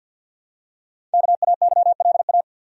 Morse code, heard as a keyed tone, about 50 wpm, 700 Hz. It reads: KAYLA
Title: USA Common Names - 50wpm